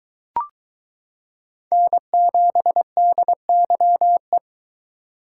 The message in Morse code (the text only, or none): N7DYE